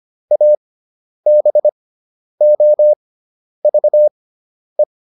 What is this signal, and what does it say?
Morse code, heard as a keyed tone, about 25 wpm, spaced as for 10 wpm, 600 Hz. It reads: ABOVE